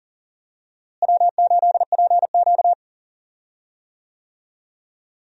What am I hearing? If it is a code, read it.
Morse code, heard as a keyed tone, about 40 wpm, 700 Hz. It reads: W8PQ
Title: USA Call Signs - 4 Characters - Repeated 2x - 40wpm